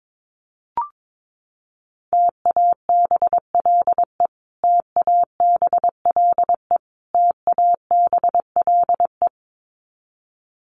Morse code, heard as a keyed tone, about 22 wpm, 700 Hz. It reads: TABLE TABLE TABLE
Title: Top 500 Words Repeated 22wpm